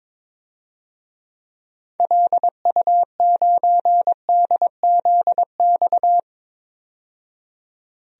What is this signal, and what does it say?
Morse code, heard as a keyed tone, about 22 wpm, 700 Hz. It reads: LU9DZX
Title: DXCC-100 Call Signs - 6 Characters - 22wpm